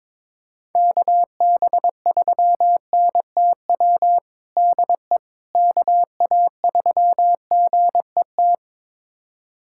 Morse code, heard as a keyed tone, about 22 wpm, 700 Hz. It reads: KB3NTW DE KA3GET
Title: USA Callsign DE Callsign - 6 Characters - 22wpm